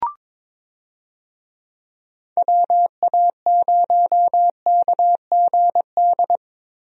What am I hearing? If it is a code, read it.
Morse code, heard as a keyed tone, about 22 wpm, 700 Hz. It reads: WA0KGD
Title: USA Call Signs - 6 Characters - Repeated 2x - 22wpm